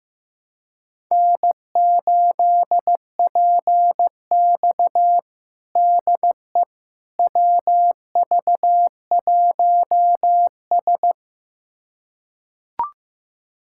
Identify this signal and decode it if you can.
Morse code, heard as a keyed tone, about 15 wpm, 700 Hz. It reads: N8PX DE WV1S
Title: USA Callsign DE Callsign - 4 Characters - 15wpm